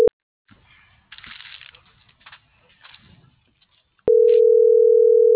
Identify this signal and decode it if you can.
Ringback tone (North America), 440 + 480 Hz.